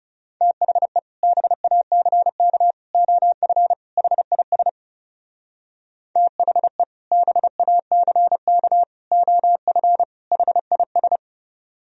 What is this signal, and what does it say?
Morse code, heard as a keyed tone, about 39 wpm, 700 Hz. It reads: THE BACK OF HIS THE BACK OF HIS